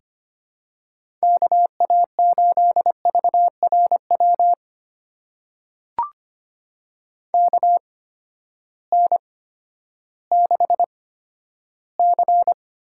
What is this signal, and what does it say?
Morse code, heard as a keyed tone, about 25 wpm, spaced as for 12 wpm, 700 Hz. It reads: KA8VRW KN6C
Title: USA Call Signs - 6 Characters - 8x Character Spacing - 25wpm